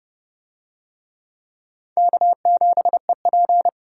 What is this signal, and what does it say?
Morse code, heard as a keyed tone, about 27 wpm, 700 Hz. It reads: K7EP